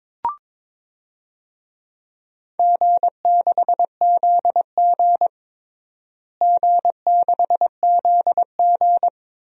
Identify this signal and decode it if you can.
Morse code, heard as a keyed tone, about 22 wpm, 700 Hz. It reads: G6ZG G6ZG
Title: DXCC-100 Call Signs - 4 Characters - Repeated 3x - 22wpm